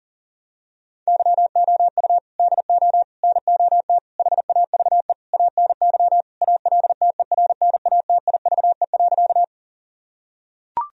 Morse code, heard as a keyed tone, about 40 wpm, 700 Hz. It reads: YOU DO NOT HAVE ANY ALTERNATIVE.
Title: Sentences from Top 1000 Words - Review Part 4 - 40wpm